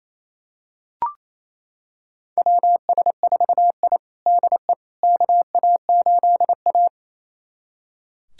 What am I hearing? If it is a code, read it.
Morse code, heard as a keyed tone, about 28 wpm, 700 Hz. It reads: WS4I DE KA8A
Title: USA Callsign DE Callsign - 4 Characters - 28wpm